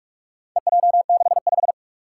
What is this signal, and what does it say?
Morse code, heard as a keyed tone, about 46 wpm, 700 Hz. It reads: EJ65
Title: Four Challenging Letter-Number Combinations 45wpm